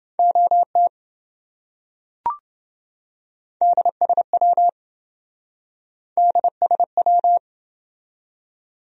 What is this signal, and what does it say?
Morse code, heard as a keyed tone, about 27 wpm, 700 Hz. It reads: OT DSW DSW